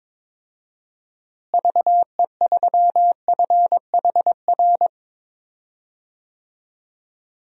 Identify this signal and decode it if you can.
Morse code, heard as a keyed tone, about 22 wpm, 700 Hz. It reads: VE3FHR
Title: DXCC-100 Call Signs - 6 Characters - Repeated 2x - 22wpm